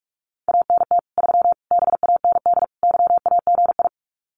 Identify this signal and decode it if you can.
Morse code, heard as a keyed tone, about 44 wpm, 700 Hz. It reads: ANT 3 BAND YAGI
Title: Common QSO Phrases 45wpm